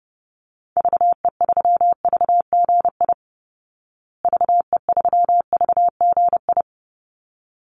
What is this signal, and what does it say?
Morse code, heard as a keyed tone, about 30 wpm, 700 Hz. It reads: VE3VGI VE3VGI
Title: Call Signs Repeated 3x - 30wpm